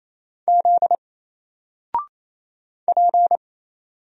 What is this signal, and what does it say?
Morse code, heard as a keyed tone, about 27 wpm, 700 Hz. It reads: Z P
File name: Single-Letter-Number-28wpm